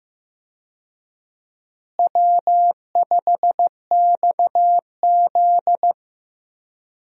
Morse code, heard as a keyed tone, about 15 wpm, 700 Hz. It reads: W5XZ
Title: USA Call Signs - 4 Characters - Repeated 2x - 15wpm